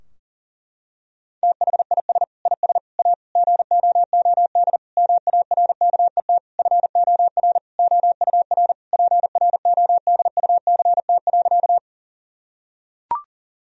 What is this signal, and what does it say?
Morse code, heard as a keyed tone, about 40 wpm, 700 Hz. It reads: THIS IS A GOOD MARKET FOR OUR PRODUCT.
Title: Sentences from Top 1000 Words - Review Part 3 - 40wpm